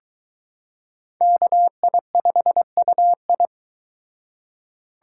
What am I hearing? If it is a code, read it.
Morse code, heard as a keyed tone, about 23 wpm, 700 Hz. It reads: KI5UI